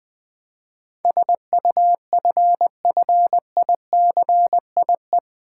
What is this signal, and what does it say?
Morse code, heard as a keyed tone, about 20 wpm, 700 Hz. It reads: SUFFICIE
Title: Build A Word - 4 Letters At A Time - 10-Letter Words - 20wpm